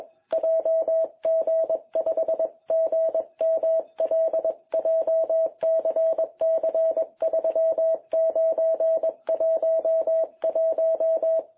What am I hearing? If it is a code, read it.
Morse code, heard as a keyed tone, about 21 wpm, 660 Hz. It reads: JG5GML JCC3911
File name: jh8cbh / 2021cw